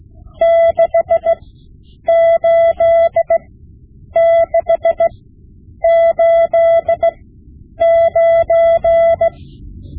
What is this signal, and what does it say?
Morse code, heard as a keyed tone, about 16 wpm, spaced as for 8 wpm, 670 Hz. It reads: 68689
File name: Morse Lessons / Morse Lessons / Tape-5B